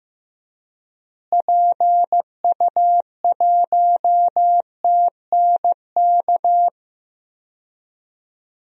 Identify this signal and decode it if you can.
Morse code, heard as a keyed tone, about 16 wpm, 700 Hz. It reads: PU1TNK
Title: DXCC-100 Call Signs - 6 Characters - Repeated 3x - 15wpm